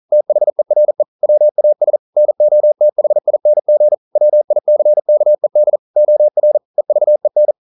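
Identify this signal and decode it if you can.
Morse code, heard as a keyed tone, about 42 wpm, 600 Hz. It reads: THERE WAS NOTHING WICKED OR EVEN
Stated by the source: Morse code practice at 41 wpm, from CW med Gnister.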